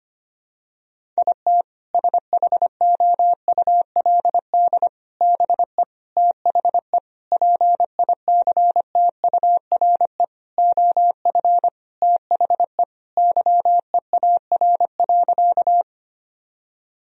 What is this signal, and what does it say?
Morse code, heard as a keyed tone, about 25 wpm, 700 Hz. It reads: IT SHOULD BE THE PICTURE OF THE YEAR.
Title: Sentences from Top 1000 Words - Review Part 1 - 25wpm